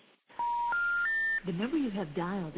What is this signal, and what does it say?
Special information tone, with speech.